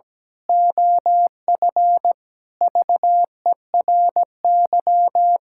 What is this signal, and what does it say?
Morse code, heard as a keyed tone, about 17 wpm, 700 Hz. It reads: OF VERY
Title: Pangram Challenge v2 - 17wpm